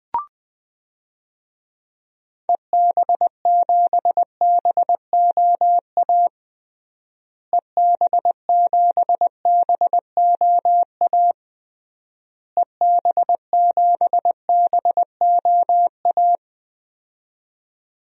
Morse code, heard as a keyed tone, about 20 wpm, 700 Hz. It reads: EB7BOA EB7BOA EB7BOA
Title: DXCC-100 Call Signs - 6 Characters - Repeated 3x - 20wpm